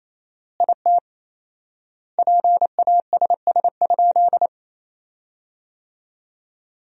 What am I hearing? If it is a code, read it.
Morse code, heard as a keyed tone, about 28 wpm, 700 Hz. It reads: IT PASS?
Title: 4x Word-Spacing - Sentences from Top 400 Words - 28wpm